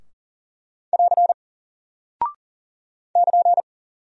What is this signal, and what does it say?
Morse code, heard as a keyed tone, about 39 wpm, 700 Hz. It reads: <AR> <KN>
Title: Single Character (Letter:Number:Punctuation:Prosign) 40wpm